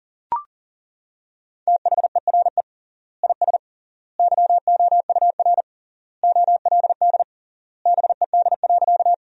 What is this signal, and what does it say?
Morse code, heard as a keyed tone, about 39 wpm, 700 Hz. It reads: THERE IS YOUR OLD BED.
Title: 3x Word-Spacing - Sentences from Top 500 Words - 40wpm